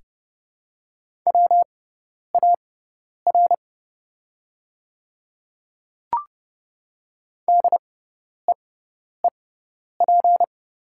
Morse code, heard as a keyed tone, about 30 wpm, spaced as for 10 wpm, 700 Hz. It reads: WAR DEEP
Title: 6x Character Spacing - Top 500 Words - 30wpm